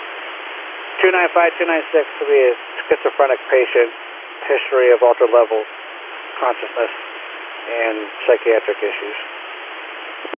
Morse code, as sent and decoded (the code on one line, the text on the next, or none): none